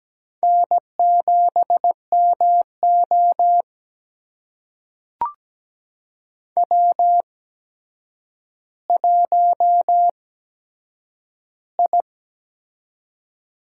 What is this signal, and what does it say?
Morse code, heard as a keyed tone, about 17 wpm, spaced as for 8 wpm, 700 Hz. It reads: N7MO W1I
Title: USA Call Signs - 4 Characters - 8x Character Spacing - 17wpm